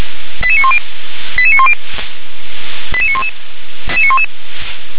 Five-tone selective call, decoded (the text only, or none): 88910 88910 88911